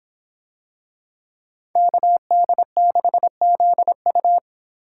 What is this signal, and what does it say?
Morse code, heard as a keyed tone, about 26 wpm, 700 Hz. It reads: KD6ZU